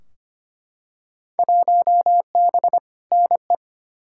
Morse code, heard as a keyed tone, about 25 wpm, 700 Hz. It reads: 1B NE
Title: Field Day Exchanges Encore - 25wpm